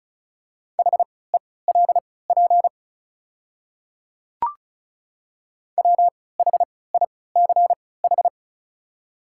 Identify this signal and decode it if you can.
Morse code, heard as a keyed tone, about 36 wpm, spaced as for 20 wpm, 700 Hz. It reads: HELP WHICH
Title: 3x Character Spacing - Top 200 Words - 35wpm